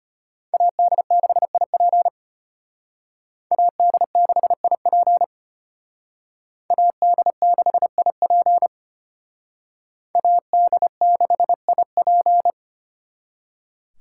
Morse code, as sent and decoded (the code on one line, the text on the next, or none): .- -.. -.... .. .--. / .- -.. -.... .. .--. / .- -.. -.... .. .--. / .- -.. -.... .. .--.
AD6IP AD6IP AD6IP AD6IP